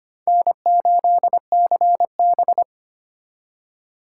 Morse code, heard as a keyed tone, about 31 wpm, 700 Hz. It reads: N8CB